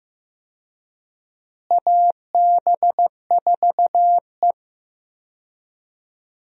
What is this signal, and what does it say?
Morse code, heard as a keyed tone, about 15 wpm, 700 Hz. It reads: AB4E
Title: USA Call Signs - 4 Characters - Repeated 3x - 15wpm